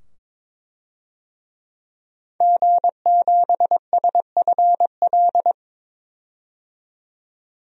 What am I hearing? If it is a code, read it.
Morse code, heard as a keyed tone, about 22 wpm, 700 Hz. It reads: G7SFL
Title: DXCC-100 Call Signs - 5 Characters - 22wpm